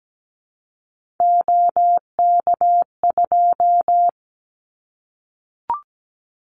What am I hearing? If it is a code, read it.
Morse code, heard as a keyed tone, about 17 wpm, 700 Hz. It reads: OK2
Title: Callsign Prefixes for Top 50 DXCC - 17wpm